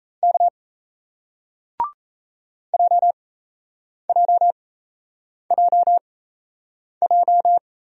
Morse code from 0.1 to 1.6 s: K

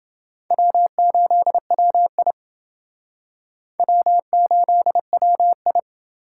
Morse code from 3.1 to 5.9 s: W8WI

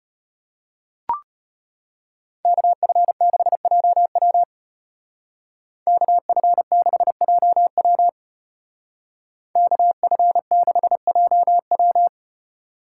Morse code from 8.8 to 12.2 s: KF6JW

